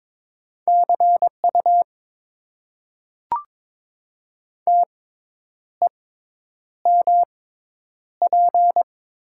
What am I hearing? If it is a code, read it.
Morse code, heard as a keyed tone, about 21 wpm, spaced as for 9 wpm, 700 Hz. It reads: CU TEMP
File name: Top 90 QSO Elements - 6x Character Spacing - 22wpm